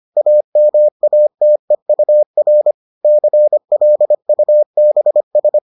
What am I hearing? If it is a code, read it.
Morse code, heard as a keyed tone, about 25 wpm, 600 Hz. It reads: AMATEUR CLUBS